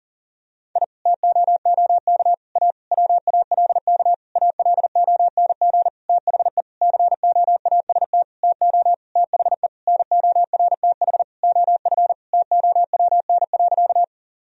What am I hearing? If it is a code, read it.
Morse code, heard as a keyed tone, about 40 wpm, 690 Hz. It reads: I TOOK A WALK ALONG THE COAST TO THE NORTH OF TOWN.
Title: Sentences from Top 1000 Words - Review Part 4 - 40wpm